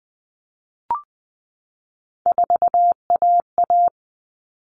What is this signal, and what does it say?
Morse code, heard as a keyed tone, about 20 wpm, 700 Hz. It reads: 4AA
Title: All Cut Numbers - 3-Digit 20wpm